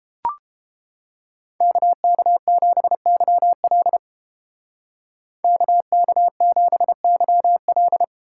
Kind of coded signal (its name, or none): Morse code